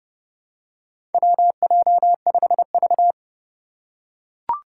Morse code, heard as a keyed tone, about 30 wpm, 700 Hz. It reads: WJ5V
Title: Four Challenging Letter-Number Combinations 30wpm